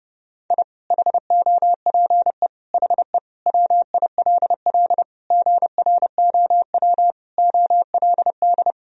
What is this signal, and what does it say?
Morse code, heard as a keyed tone, about 30 wpm, 700 Hz. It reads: I HOPE HE WILL GROW OLD
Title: Sentences from Top 1000 Words - Review Part 1 - 30wpm